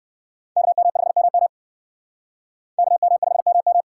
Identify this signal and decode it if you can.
Morse code, heard as a keyed tone, about 56 wpm, 700 Hz. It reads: KM5MM KM5MM
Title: Speed Racing - USA Call Signs 45wpm